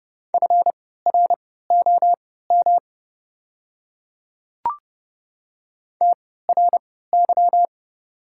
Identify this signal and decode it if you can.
Morse code, heard as a keyed tone, about 29 wpm, spaced as for 17 wpm, 700 Hz. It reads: FROM TRY
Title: 3x Character Spacing - Top 200 Words - 30wpm